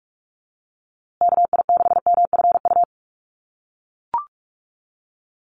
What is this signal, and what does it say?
Morse code, heard as a keyed tone, about 44 wpm, 700 Hz. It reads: KI6MFU
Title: USA Call Signs Repeated 3x 45wpm